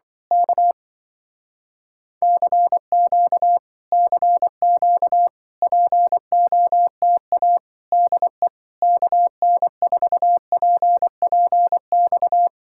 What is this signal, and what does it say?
Morse code, heard as a keyed tone, about 24 wpm, 700 Hz. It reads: K CQ CQ POTA DE KN4PPX
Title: Speed-Racing Common QSO Phrases - 20wpm